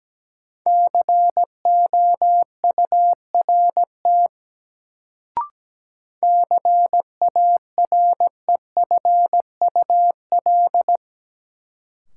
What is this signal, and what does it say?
Morse code, heard as a keyed tone, about 17 wpm, 700 Hz. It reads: COURT CAREFUL
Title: Sentences from Top 1600 Words - New Word Drill Alt - 17wpm